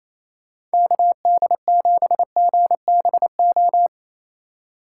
Morse code, heard as a keyed tone, about 34 wpm, 700 Hz. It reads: KD7GBO
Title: Speed Racing - USA 2x3 Call Signs 28wpm